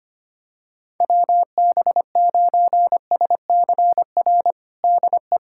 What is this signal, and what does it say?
Morse code, heard as a keyed tone, about 25 wpm, 700 Hz. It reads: WB9SCR DE
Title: USA Callsign DE Callsign 25wpm